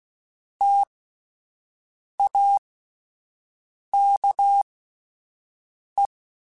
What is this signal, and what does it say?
Morse code, heard as a keyed tone, about 16 wpm, spaced as for 5 wpm, 800 Hz. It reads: TAKE
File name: Alphabetical / Alphabetical mp3s / Ts @5